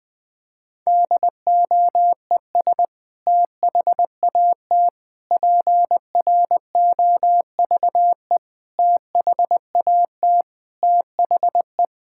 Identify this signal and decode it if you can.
Morse code, heard as a keyed tone, about 21 wpm, 700 Hz. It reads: DOES THAT PROVE THAT THE
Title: Sentences from Top 1600 Words 20wpm